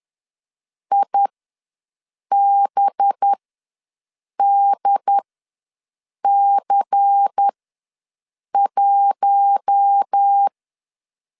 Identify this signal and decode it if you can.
Morse code, heard as a keyed tone, about 10 wpm, spaced as for 6 wpm, 800 Hz. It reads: IBDC1
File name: LEZIONI / 8 YJ45 / 8 Y ws 10wpm 3952